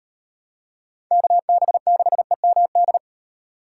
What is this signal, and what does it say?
Morse code, heard as a keyed tone, about 38 wpm, 700 Hz. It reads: KB6EMD